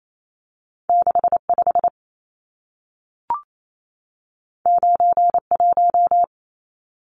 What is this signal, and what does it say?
Morse code, heard as a keyed tone, about 28 wpm, 700 Hz. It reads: 65 91
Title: Numbers - 2-Digit 28wpm